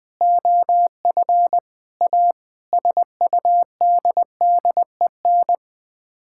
Morse code, heard as a keyed tone, about 20 wpm, 700 Hz. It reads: OF A SUDDEN
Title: Sets of 6 Words - 20wpm